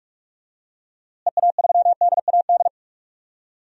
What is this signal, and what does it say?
Morse code, heard as a keyed tone, about 46 wpm, 700 Hz. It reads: EA3DAD
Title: DXCC-100 Call Signs - 6 Characters - Repeated 3x - 45wpm